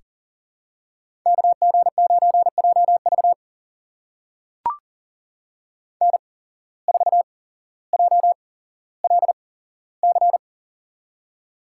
Morse code, heard as a keyed tone, about 40 wpm, spaced as for 17 wpm, 700 Hz. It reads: KG9JV N4JLC
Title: USA Call Signs - 5 Characters - 8x Character Spacing - 40wpm